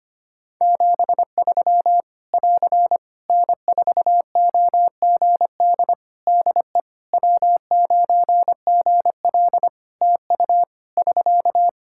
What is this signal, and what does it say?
Morse code, heard as a keyed tone, about 25 wpm, 700 Hz. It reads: 73 <AR> N4OGD DE W9GL TU <SK>
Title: K1USN SST QSO Practice 25wpm